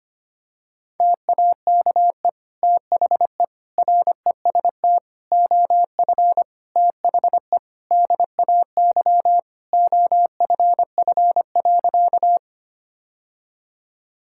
Morse code, heard as a keyed tone, about 25 wpm, 700 Hz. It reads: TAKE THE REST OF THE DAY OFF.
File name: practice / 25wpm / Sentences with 2-Word Phrases - 25wpm